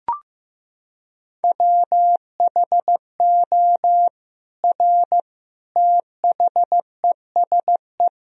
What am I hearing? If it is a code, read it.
Morse code, heard as a keyed tone, about 15 wpm, 700 Hz. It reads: WHO R THESE
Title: Sentences from Top 200 Words and Abbreviations 15wpm